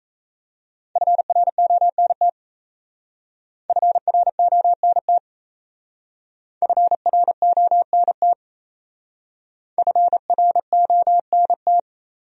Morse code, heard as a keyed tone, about 34 wpm, 700 Hz. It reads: FRONT FRONT FRONT FRONT
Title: Speed-Racing Top 500 Words 28wpm